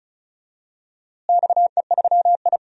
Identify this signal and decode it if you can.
Morse code, heard as a keyed tone, about 34 wpm, 700 Hz. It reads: XE3I
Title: DXCC-100 Call Signs - 4 Characters - Repeated 3x - 35wpm